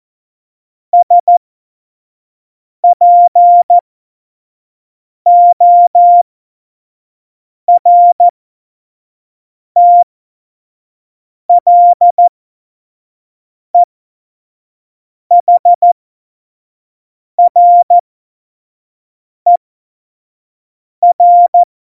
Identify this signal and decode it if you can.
Morse code, heard as a keyed tone, about 14 wpm, spaced as for 5 wpm, 700 Hz. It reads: SPORTLEHRER